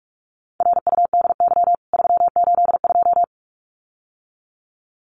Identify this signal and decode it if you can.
Morse code, heard as a keyed tone, about 44 wpm, 700 Hz. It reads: RUDY 382
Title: CWT Exchange 45wpm